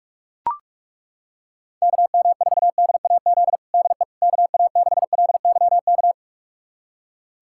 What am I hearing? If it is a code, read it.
Morse code, heard as a keyed tone, about 41 wpm, 700 Hz. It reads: KM4DAZ DE KA6LYK
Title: USA Callsign DE Callsign 45wpm